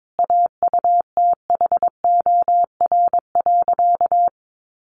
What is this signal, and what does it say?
Morse code, heard as a keyed tone, about 22 wpm, 700 Hz. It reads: AUTHOR.
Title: Sentences from Top 1200 Words 22wpm